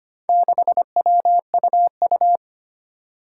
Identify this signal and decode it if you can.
Morse code, heard as a keyed tone, about 25 wpm, 700 Hz. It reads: 6WUU
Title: Four Challenging Letter-Number Combinations 25wpm